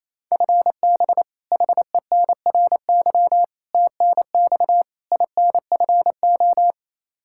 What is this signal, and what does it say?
Morse code, heard as a keyed tone, about 27 wpm, 700 Hz. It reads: FB HENRY TNX INFO
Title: Practice QSO - Exchange 3 - More Info - 28wpm